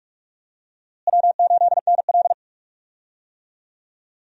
Morse code, heard as a keyed tone, about 42 wpm, 700 Hz. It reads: W8NL
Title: USA Call Signs - 4 Characters - Repeated 3x - 45wpm